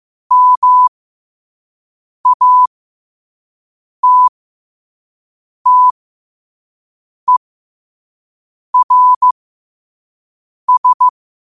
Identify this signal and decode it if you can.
Morse code, heard as a keyed tone, about 15 wpm, spaced as for 5 wpm, 1000 Hz. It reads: MATTERS